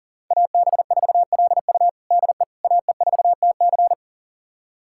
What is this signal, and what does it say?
Morse code, heard as a keyed tone, about 37 wpm, 700 Hz. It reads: AB4LU DE AE4TC
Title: USA Callsign DE Callsign - 5 Characters - 40wpm